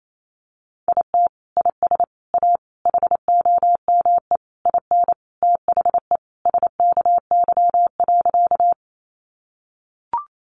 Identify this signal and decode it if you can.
Morse code, heard as a keyed tone, about 28 wpm, 700 Hz. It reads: IT IS A HOME IN THE SKY.
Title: Sentences from Top 1200 Words 28wpm